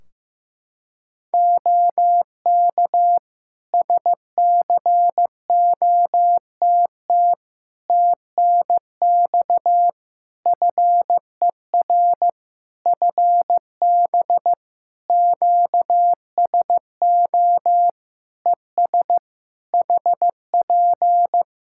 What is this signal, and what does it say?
Morse code, heard as a keyed tone, about 15 wpm, 700 Hz. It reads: OK SCOTT TNX FER FB QSO ES HP
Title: Practice QSO - Exchange 4 - Ending - 15wpm